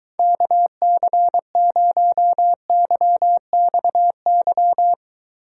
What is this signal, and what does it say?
Morse code, heard as a keyed tone, about 24 wpm, 700 Hz. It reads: KC0YXY